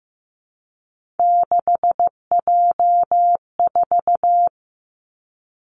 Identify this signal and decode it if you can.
Morse code, heard as a keyed tone, about 15 wpm, 700 Hz. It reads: 6J4
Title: Three Challenging Letter-Number Combinations 15wpm